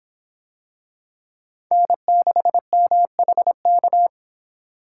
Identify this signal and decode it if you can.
Morse code, heard as a keyed tone, about 26 wpm, 700 Hz. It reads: N6MHK